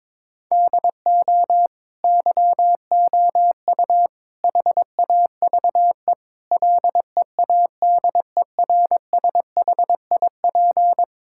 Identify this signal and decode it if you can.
Morse code, heard as a keyed tone, about 22 wpm, 700 Hz. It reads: DO YOU HAVE LEADERSHIP
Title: Sentences from Top 1600 Words 22wpm